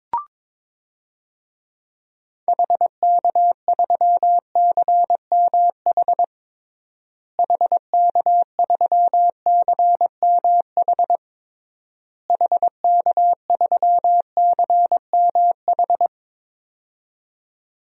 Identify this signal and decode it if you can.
Morse code, heard as a keyed tone, about 22 wpm, 700 Hz. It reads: HK3CMH HK3CMH HK3CMH
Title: DXCC-100 Call Signs - 6 Characters - Repeated 3x - 22wpm